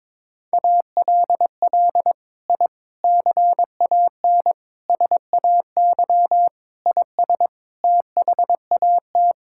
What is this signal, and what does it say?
Morse code, heard as a keyed tone, about 22 wpm, 700 Hz. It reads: ALL I CAN SAY IS THAT
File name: practice / 22wpm / Sets of 6 Words - 22wpm